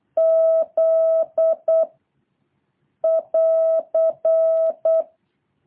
Morse code, heard as a keyed tone, about 8 wpm, 660 Hz. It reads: Z <AR>
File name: jh8cbh / 2023cw